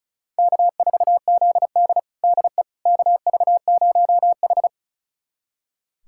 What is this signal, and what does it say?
Morse code, heard as a keyed tone, about 34 wpm, 700 Hz. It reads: K4ZD DE KV0H
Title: USA Callsign DE Callsign - 4 Characters - 35wpm